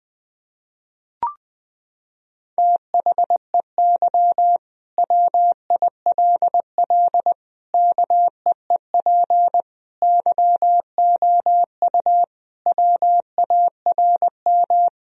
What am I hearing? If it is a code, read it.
Morse code, heard as a keyed tone, about 20 wpm, 700 Hz. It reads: THEY WILL KEEP YOU WARM
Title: Sentences from Top 1000 Words - Review Part 3 - 20wpm